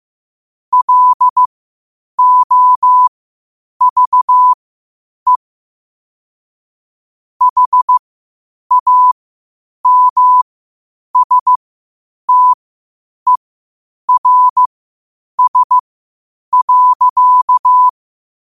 Morse code, heard as a keyed tone, about 15 wpm, spaced as for 8 wpm, 1000 Hz. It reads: LOVE HAMSTERS.
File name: qotdfiles / 8wpm